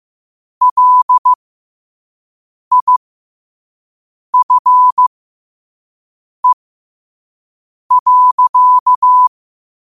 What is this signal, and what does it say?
Morse code, heard as a keyed tone, about 15 wpm, spaced as for 5 wpm, 1000 Hz. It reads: LIFE.
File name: qotdfiles / 5wpm